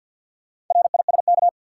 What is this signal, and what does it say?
Morse code, heard as a keyed tone, about 52 wpm, 700 Hz. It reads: RISK
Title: Sentences from Top 600 Words - New Word Drill - 50wpm